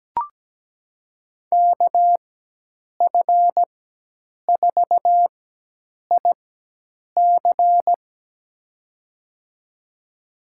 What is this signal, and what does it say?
Morse code, heard as a keyed tone, about 17 wpm, spaced as for 8 wpm, 700 Hz. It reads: KF4IC
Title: USA Call Signs - 4x Character Spacing - 17wpm